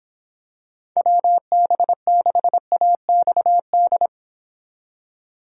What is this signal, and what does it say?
Morse code, heard as a keyed tone, about 26 wpm, 700 Hz. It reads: WB6AXD